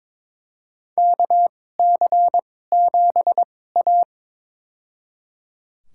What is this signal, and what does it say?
Morse code, heard as a keyed tone, about 22 wpm, spaced as for 16 wpm, 700 Hz. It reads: KC7A
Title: USA Call Signs - 4 Characters - 2x Character Spacing - 22wpm